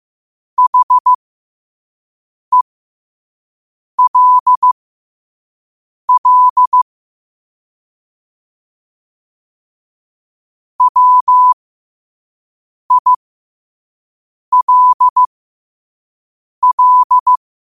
Morse code, heard as a keyed tone, about 15 wpm, spaced as for 5 wpm, 1000 Hz. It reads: HELL WILL
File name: qotdfiles / 5wpm